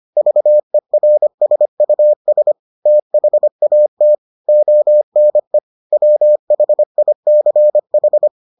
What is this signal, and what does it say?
Morse code, heard as a keyed tone, about 25 wpm, 600 Hz. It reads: VERSUS THAT ONE WHICH